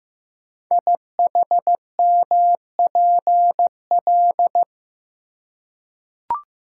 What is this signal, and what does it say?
Morse code, heard as a keyed tone, about 15 wpm, 700 Hz. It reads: IHMPL
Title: Code Groups - 5 Letters - 15wpm